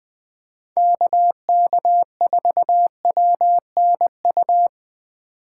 Morse code, heard as a keyed tone, about 20 wpm, 700 Hz. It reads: KK4WNU
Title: Speed Racing - USA 2x3 Call Signs 17wpm